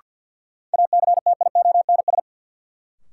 Morse code, heard as a keyed tone, about 52 wpm, 700 Hz. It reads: ACTIONS
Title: Sentences from Top 1000 Words - Review Part 6 - New Word Drill - 50wpm